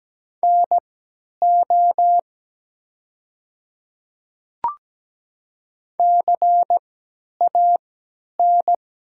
Morse code, heard as a keyed tone, about 17 wpm, spaced as for 10 wpm, 700 Hz. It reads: NO CAN
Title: 3x Character Spacing - Top 100 Words - 17wpm